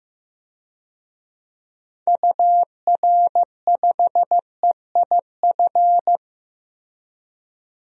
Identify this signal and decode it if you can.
Morse code, heard as a keyed tone, about 16 wpm, 700 Hz. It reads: UR5EIF
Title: DXCC-100 Call Signs - 6 Characters - 15wpm